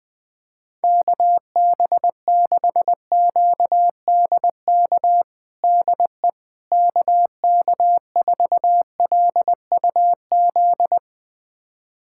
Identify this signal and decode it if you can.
Morse code, heard as a keyed tone, about 21 wpm, 700 Hz. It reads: KB6QDK DE KK4LUZ
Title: USA Callsign DE Callsign 20wpm